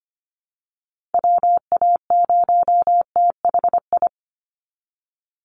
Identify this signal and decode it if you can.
Morse code, heard as a keyed tone, about 25 wpm, 700 Hz. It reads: WA0THI
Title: USA Call Signs 25wpm